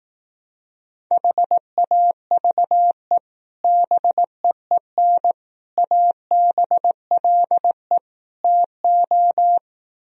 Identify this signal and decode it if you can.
Morse code, heard as a keyed tone, about 18 wpm, 700 Hz. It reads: HAVE BEEN ABLE TO
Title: Speed-Racing - Sets of 4 Words 15wpm